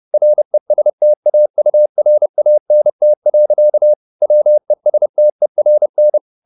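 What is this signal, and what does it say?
Morse code, heard as a keyed tone, about 30 wpm, 600 Hz. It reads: RESTAURANT. WESTERN
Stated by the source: Morse code at 30 wpm, the novel Never by Ken Follett in English, from CW med Gnister.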